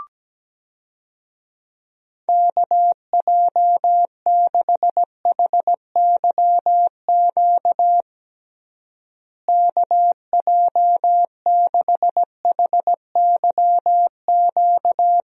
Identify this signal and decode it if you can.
Morse code, heard as a keyed tone, about 17 wpm, 700 Hz. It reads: KJ6HYQ KJ6HYQ
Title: USA Call Signs - 6 Characters - Repeated 2x - 17wpm